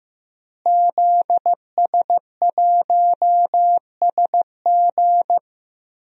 Morse code, heard as a keyed tone, about 15 wpm, 700 Hz. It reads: ZS1SG
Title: DXCC-100 Call Signs - 5 Characters - Repeated 2x - 15wpm